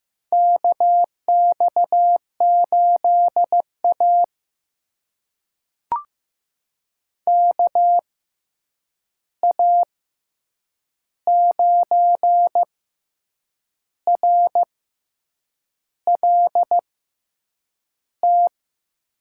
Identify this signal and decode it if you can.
Morse code, heard as a keyed tone, about 15 wpm, spaced as for 7 wpm, 700 Hz. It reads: KX8A KA9RLT
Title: USA Call Signs - 6x Character Spacing - 15wpm